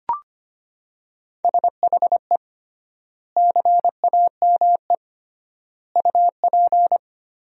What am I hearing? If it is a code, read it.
Morse code, heard as a keyed tone, about 25 wpm, 700 Hz. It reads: SHE CAME UP
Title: 3x Word-Spacing - Sentences from Top 200 Words - 25wpm